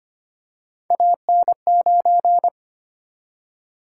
Morse code, heard as a keyed tone, about 25 wpm, 700 Hz. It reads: AN9
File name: Three Letter-Number Combinations - 25wpm